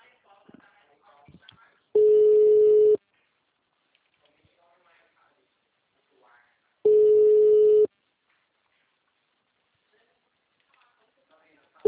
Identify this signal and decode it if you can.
Ringback tone (Europe, 425 Hz), with speech.